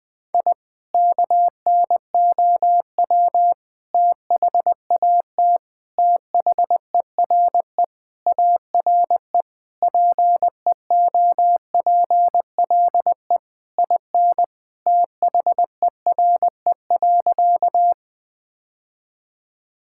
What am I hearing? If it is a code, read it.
Morse code, heard as a keyed tone, about 20 wpm, 700 Hz. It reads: I KNOW THAT THERE ARE PEOPLE IN THERE.
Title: Sentences with 3-Word Phrases - 20wpm